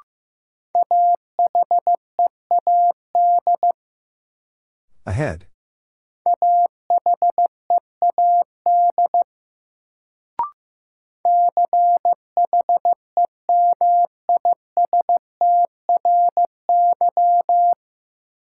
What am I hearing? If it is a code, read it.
Morse code, heard as a keyed tone, about 15 wpm, 700 Hz. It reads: AHEAD AHEAD CHEMISTRY
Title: Sentences from Top 1000 Words - Review Part 6 - New Word Drill - 15wpm